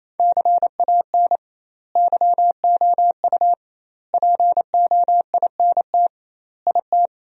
Morse code, heard as a keyed tone, about 28 wpm, 700 Hz. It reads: CAN YOU POINT IT
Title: 2x Word-Spacing - Sentences from Top 200 Words - 28wpm